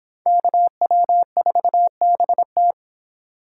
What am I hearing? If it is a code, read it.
Morse code, heard as a keyed tone, about 26 wpm, 700 Hz. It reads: KW4BT